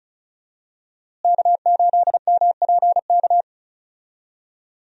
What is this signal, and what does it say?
Morse code, heard as a keyed tone, about 36 wpm, 700 Hz. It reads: K8MPK
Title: USA Call Signs - 5 Characters - Repeated 2x - 35wpm